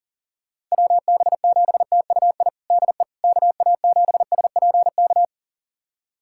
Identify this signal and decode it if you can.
Morse code, heard as a keyed tone, about 40 wpm, 700 Hz. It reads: WB7TUI DE KA7SPK
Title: USA Callsign DE Callsign 40wpm